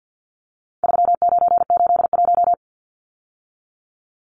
Morse code, heard as a keyed tone, about 50 wpm, 700 Hz. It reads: 3981
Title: Numbers - 4-Digit 50wpm